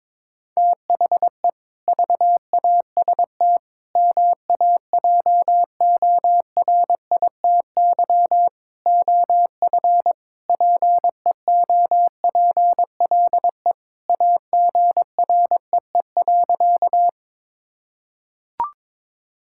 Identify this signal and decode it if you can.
Morse code, heard as a keyed tone, about 22 wpm, 700 Hz. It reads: THE VAST MAJORITY OF PEOPLE AGREE.
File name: practice / 22wpm / Sentences with 4-Word Phrases - 22wpm